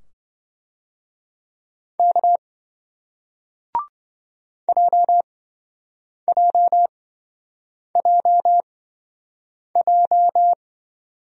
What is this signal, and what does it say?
Morse code, heard as a keyed tone, about 27 wpm, spaced as for 7 wpm, 700 Hz. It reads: K JJJJ